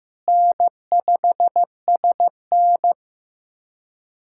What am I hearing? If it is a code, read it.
Morse code, heard as a keyed tone, about 15 wpm, 700 Hz. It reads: N5SN
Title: USA Callsign DE Callsign - 4 Characters - 15wpm